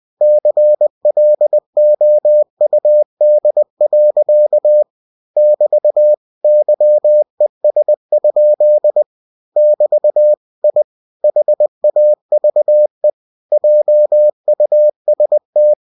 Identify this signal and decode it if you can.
Morse code, heard as a keyed tone, about 20 wpm, 600 Hz. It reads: CLOUD. <BT> YES? <BT> I HAVE JUST
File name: cw-books / winnie-the-pooh / winnie-20wpm-page-0009